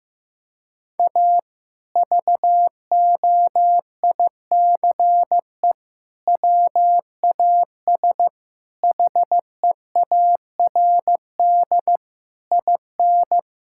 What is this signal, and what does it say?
Morse code, heard as a keyed tone, about 15 wpm, 700 Hz. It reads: A VOICE WAS HEARD IN
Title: Sentences from Top 1000 Words - Review Part 5 - 15wpm